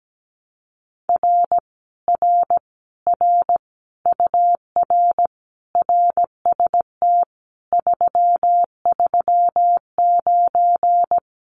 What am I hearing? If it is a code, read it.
Morse code, heard as a keyed tone, about 17 wpm, 700 Hz. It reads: R R R UR RST 339
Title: Common QSO Phrases Encore 17wpm